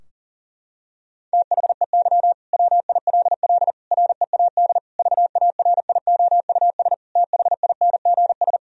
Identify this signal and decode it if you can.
Morse code, heard as a keyed tone, about 40 wpm, 700 Hz. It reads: THEY WILL READ VARIOUS THINGS
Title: Sentences from Top 1000 Words - Review Part 4 - 40wpm